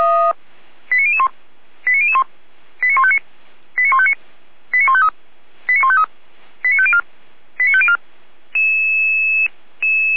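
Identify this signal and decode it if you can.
Five-tone selective call (ZVEI), heard as quick runs of five tones, tones of about 80 ms. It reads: 89001 89001 89159 89159 89153 89153 89504 89504
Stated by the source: ZVEI five-tone alert received by SDR, from Feuerwehr Orlamünde's audio archive.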